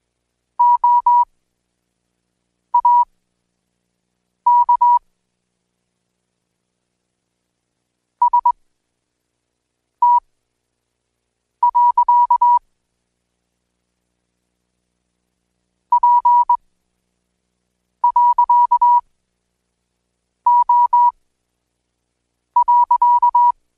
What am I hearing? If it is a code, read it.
Morse code, heard as a keyed tone, about 20 wpm, spaced as for 5 wpm, 970 Hz. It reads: OAK ST. P.O.
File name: Code Quick / AudioDownld1 / Seg76-Day8-DayDrill 8